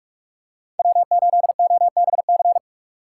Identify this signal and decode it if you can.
Morse code, heard as a keyed tone, about 44 wpm, 700 Hz. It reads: W8OBC